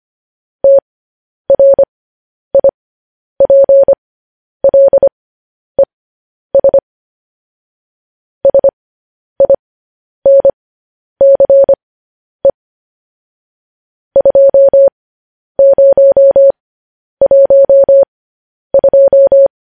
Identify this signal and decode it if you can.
Morse code, heard as a keyed tone, about 25 wpm, spaced as for 10 wpm, 550 Hz. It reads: TRIPLES SINCE 2012
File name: podcast / News headlines-1x-25x10